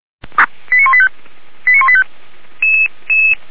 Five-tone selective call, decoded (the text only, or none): 89185 89185